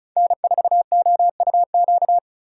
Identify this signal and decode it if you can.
Morse code, heard as a keyed tone, about 34 wpm, 690 Hz. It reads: N4OUQ
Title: USA Call Signs - 5 Characters - 3x Character Spacing - 35wpm